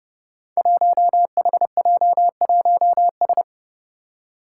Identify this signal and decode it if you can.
Morse code, heard as a keyed tone, about 28 wpm, 700 Hz. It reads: 1HJ1S
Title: Five Challenging Letter-Number Combinations 30wpm